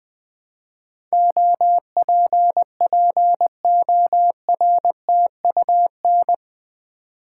Morse code, heard as a keyed tone, about 20 wpm, 700 Hz. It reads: OPPORTUN
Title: Build A Word - 4 Letters At A Time - 11-Letter Words - 20wpm